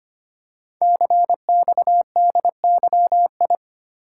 Morse code, heard as a keyed tone, about 25 wpm, 700 Hz. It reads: CXDYI
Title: Code Groups - 5 Letters - 25wpm